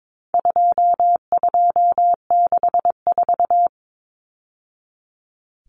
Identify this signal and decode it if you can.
Morse code, heard as a keyed tone, about 22 wpm, 700 Hz. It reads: 2264
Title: Numbers - 4-Digit 22wpm